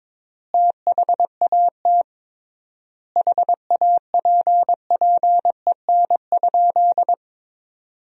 Morse code, heard as a keyed tone, about 22 wpm, 700 Hz. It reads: THAT HAPPEN?
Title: 3x Word-Spacing - Sentences from Top 400 Words - 22wpm